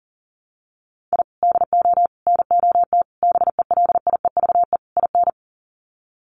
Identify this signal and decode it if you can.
Morse code, heard as a keyed tone, about 40 wpm, 700 Hz. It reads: I DO NOT BELIEVE IN
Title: Sets of 5 Words Encore 40wpm